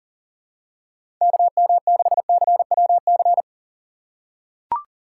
Morse code, heard as a keyed tone, about 39 wpm, 700 Hz. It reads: KM6CWC
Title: USA Call Signs - 6 Characters - 6x Character Spacing - 40wpm